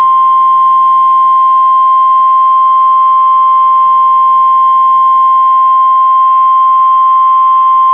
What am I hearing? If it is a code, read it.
Emergency Alert System (SAME) tones (the 1050 Hz attention tone). It is heard from 0.4 to 7.4 s.